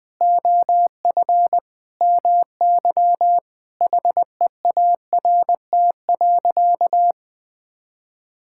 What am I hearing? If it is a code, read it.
Morse code, heard as a keyed tone, about 20 wpm, 700 Hz. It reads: OF MY HEART.
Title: Sentences with 8-Word Phrases - 20wpm